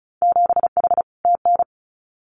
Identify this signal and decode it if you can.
Morse code, heard as a keyed tone, about 34 wpm, 700 Hz. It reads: 7H TN
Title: Winter Field Day Exchanges Encore 35wpm